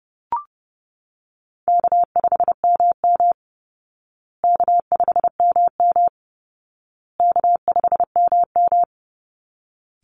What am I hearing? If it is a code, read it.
Morse code, heard as a keyed tone, about 30 wpm, 700 Hz. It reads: K5MM K5MM K5MM
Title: Easy Call Signs Repeated 3x - 30wpm